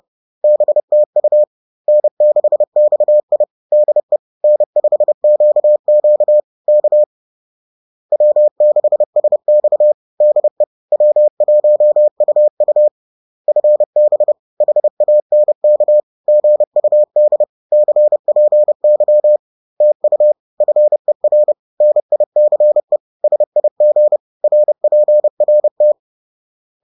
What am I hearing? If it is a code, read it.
Morse code, heard as a keyed tone, about 30 wpm, 600 Hz. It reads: BTU N6XI DE N5QQ K W6SX DE W1UU FB HANK GUD CPY TU FER NICE SIG RPRT
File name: 30wpm-4